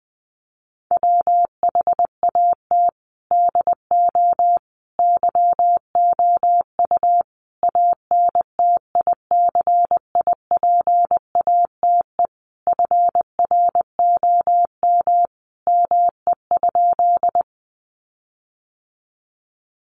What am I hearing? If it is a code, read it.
Morse code, heard as a keyed tone, about 20 wpm, 700 Hz. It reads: WHAT DO YOU ANTICIPATE FROM ME?
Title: Sentences from Top 1500 Words 20wpm